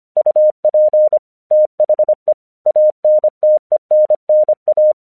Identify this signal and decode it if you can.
Morse code, heard as a keyed tone, about 25 wpm, 610 Hz. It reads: UP THE ANTENNA
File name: 1 English 25wpm / AQC27